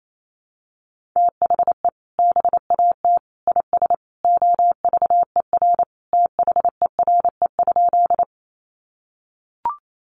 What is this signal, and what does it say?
Morse code, heard as a keyed tone, about 28 wpm, 700 Hz. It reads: THE BAT IS OVER THERE?
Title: Sentences from Top 1500 Words 28wpm